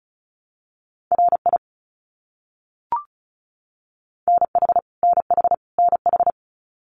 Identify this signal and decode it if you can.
Morse code, heard as a keyed tone, about 34 wpm, 700 Hz. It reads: RI NH NH NH
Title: US States Repeated 3x 35wpm